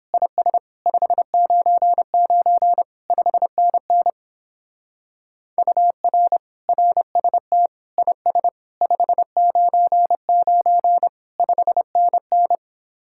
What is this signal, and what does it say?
Morse code, heard as a keyed tone, about 31 wpm, 700 Hz. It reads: IS 599 5NN UR RST IS 599 5NN